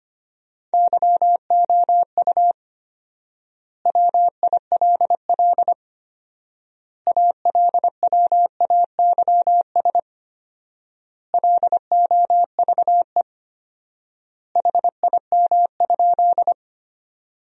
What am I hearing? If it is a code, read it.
Morse code, heard as a keyed tone, about 25 wpm, 700 Hz. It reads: YOU WILL ALWAYS LOVE HIM?
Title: 4x Word-Spacing - Sentences from Top 400 Words - 25wpm